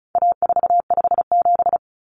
Morse code, heard as a keyed tone, about 34 wpm, 700 Hz. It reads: A457
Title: Common Cut Numbers - 4-Digit 35wpm